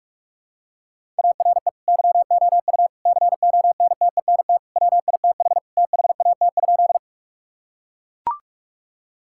Morse code, heard as a keyed tone, about 41 wpm, 700 Hz. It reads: ARE YOU CONTENT WITH THAT?
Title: Sentences from Top 1000 Words - Review Part 6 - 45wpm